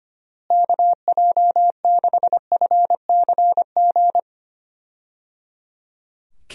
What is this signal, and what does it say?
Morse code, heard as a keyed tone, about 25 wpm, 700 Hz. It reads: KJ6FCG
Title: USA Call Signs - 6 Characters - Repeated 3x - 25wpm